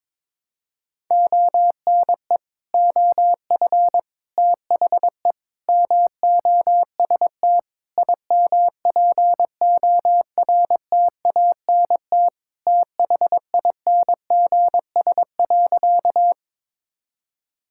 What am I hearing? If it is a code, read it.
Morse code, heard as a keyed tone, about 22 wpm, 700 Hz. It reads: ONE OF THE MOST IMPORTANT THINGS.
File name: practice / 22wpm / Sentences with 4-Word Phrases - 22wpm